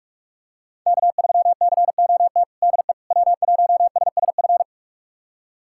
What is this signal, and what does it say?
Morse code, heard as a keyed tone, about 46 wpm, 700 Hz. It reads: K3COT DE W1SSF
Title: USA Callsign DE Callsign 45wpm